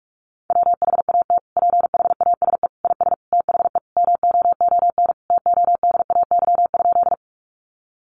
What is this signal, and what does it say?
Morse code, heard as a keyed tone, about 44 wpm, 700 Hz. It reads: WHAT PHASE IS THE MOON TODAY?
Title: Sentences from Top 1200 Words 45wpm